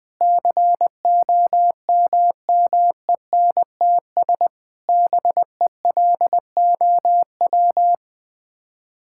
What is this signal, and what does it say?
Morse code, heard as a keyed tone, about 20 wpm, 700 Hz. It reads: COMMENTS BELOW
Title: Sets of 9 Words - 20wpm